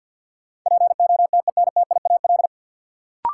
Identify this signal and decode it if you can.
Morse code, heard as a keyed tone, about 53 wpm, 700 Hz. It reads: POTENTIAL
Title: Sentences from Top 1000 Words - Review Part 4 - New Word Drill - 50wpm